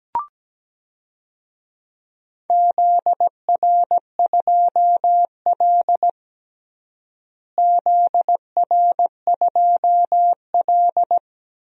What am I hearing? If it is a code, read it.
Morse code, heard as a keyed tone, about 17 wpm, 700 Hz. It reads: ZR2L ZR2L
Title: DXCC-100 Call Signs - 4 Characters - Repeated 3x - 17wpm